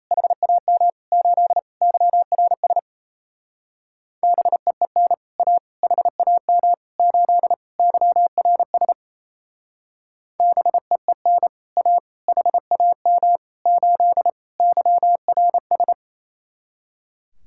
Morse code, heard as a keyed tone, about 34 wpm, 700 Hz. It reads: HAM 8 YRS BEEN A HAM 8 YRS BEEN A HAM 8 YRS
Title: Speed-Racing Common QSO Phrases - 28wpm